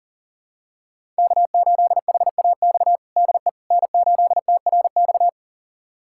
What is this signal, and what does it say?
Morse code, heard as a keyed tone, about 40 wpm, 700 Hz. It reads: K8HAX DE N8TRX
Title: USA Callsign DE Callsign - 5 Characters - 40wpm